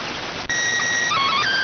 SSTV (slow-scan television), PD 290.